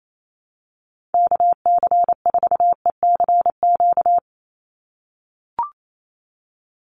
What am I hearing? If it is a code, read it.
Morse code, heard as a keyed tone, about 28 wpm, 700 Hz. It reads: KC4ECQ
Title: USA Call Signs 28wpm